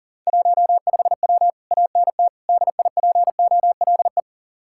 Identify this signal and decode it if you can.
Morse code, heard as a keyed tone, about 53 wpm, 700 Hz. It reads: 15W ANT DIPOLE